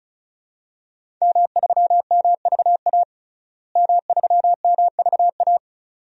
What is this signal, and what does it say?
Morse code, heard as a keyed tone, about 36 wpm, 700 Hz. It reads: M3MVA M3MVA
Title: DXCC-100 Call Signs - 5 Characters - Repeated 2x - 35wpm